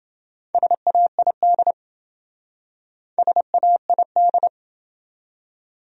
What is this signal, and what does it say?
Morse code, heard as a keyed tone, about 27 wpm, 700 Hz. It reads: SAID SAID